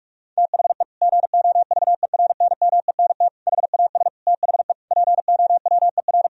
Morse code, heard as a keyed tone, about 46 wpm, 700 Hz. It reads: THE GOVERNMENT HAS THE POWER
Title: Sentences from Top 1000 Words - Review Part 3 - 45wpm